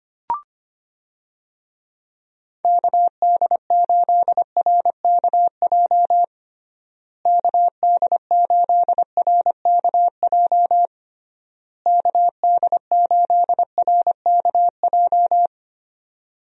Morse code, heard as a keyed tone, about 25 wpm, 700 Hz. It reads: KD8RKJ KD8RKJ KD8RKJ
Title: USA Call Signs - 6 Characters - Repeated 3x - 25wpm